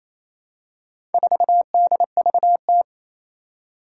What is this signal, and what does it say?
Morse code, heard as a keyed tone, about 28 wpm, 700 Hz. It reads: 4DVT
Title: Four Challenging Letter-Number Combinations 28wpm